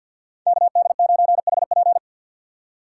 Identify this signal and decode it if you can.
Morse code, heard as a keyed tone, about 53 wpm, 700 Hz. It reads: KD9HP
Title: USA Call Signs - 5 Characters - 6x Character Spacing - 50wpm